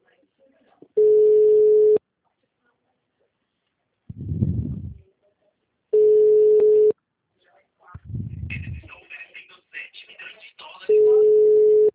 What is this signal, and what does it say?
Ringback tone (Europe, 425 Hz), with speech.
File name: out- -2015- - - .48146